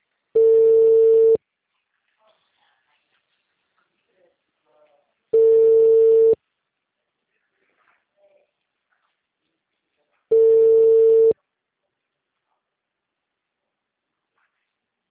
Ringback tone (China, 450 Hz).